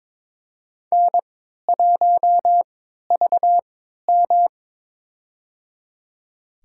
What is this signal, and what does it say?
Morse code, heard as a keyed tone, about 22 wpm, spaced as for 12 wpm, 700 Hz. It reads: N1VM
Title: USA Call Signs - 3x Character Spacing - 22wpm